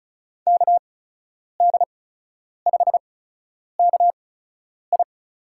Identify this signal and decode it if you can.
Morse code, heard as a keyed tone, about 34 wpm, spaced as for 10 wpm, 700 Hz. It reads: KD5KI
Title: USA Call Signs - 6 Characters - 8x Character Spacing - 35wpm